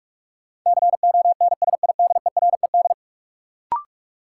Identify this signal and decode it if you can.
Morse code, heard as a keyed tone, about 46 wpm, 700 Hz. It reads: CONSIDERED
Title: Sets of 5 Words - New Word Drill - 45wpm